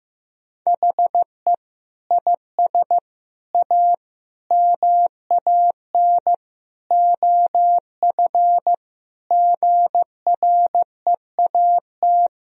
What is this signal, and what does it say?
Morse code, heard as a keyed tone, about 15 wpm, 700 Hz. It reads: HE IS A MAN OF GREAT
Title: Sentences from Top 1000 Words - Review Part 3 - 15wpm